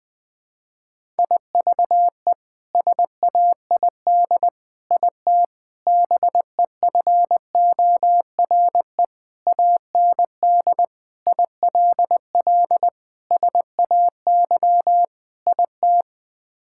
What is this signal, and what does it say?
Morse code, heard as a keyed tone, about 21 wpm, 700 Hz. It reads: IVE SAID IT BEFORE AND ILL SAY IT
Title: Sets of 8 Words - 20wpm